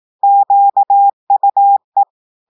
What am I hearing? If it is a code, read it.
Morse code, heard as a keyed tone, about 18 wpm, 800 Hz. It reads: QUE
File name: audios / portugais / portugais texte en clair 1 vitesse 18 mots minutes